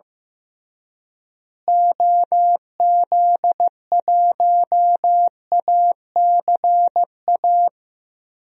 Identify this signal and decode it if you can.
Morse code, heard as a keyed tone, about 15 wpm, 700 Hz. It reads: OZ1ACA
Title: DXCC-100 Call Signs - 6 Characters - Repeated 3x - 15wpm